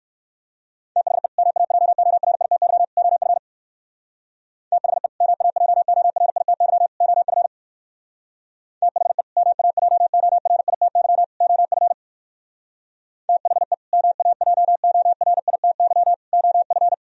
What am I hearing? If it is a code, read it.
Morse code, heard as a keyed tone, about 59 wpm, 700 Hz. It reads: THE MAJORITY OF THE MAJORITY OF THE MAJORITY OF THE MAJORITY OF